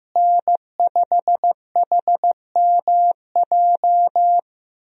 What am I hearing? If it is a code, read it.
Morse code, heard as a keyed tone, about 15 wpm, 700 Hz. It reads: N5HMJ
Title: USA Call Signs - 5 Characters - Repeated 3x - 15wpm